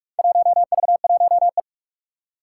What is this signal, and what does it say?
Morse code, heard as a keyed tone, about 46 wpm, 700 Hz. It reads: 1V1E
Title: Four Challenging Letter-Number Combinations 45wpm